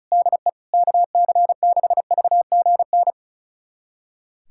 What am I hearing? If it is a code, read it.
Morse code, heard as a keyed tone, about 34 wpm, 700 Hz. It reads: DE KC6VGN
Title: USA Callsign DE Callsign 35wpm